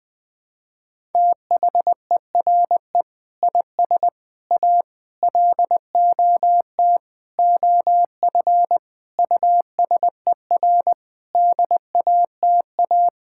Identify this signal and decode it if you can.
Morse code, heard as a keyed tone, about 20 wpm, 700 Hz. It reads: THERE IS A LOT OF USER DATA
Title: Sentences from Top 1000 Words - Review Part 6 - 20wpm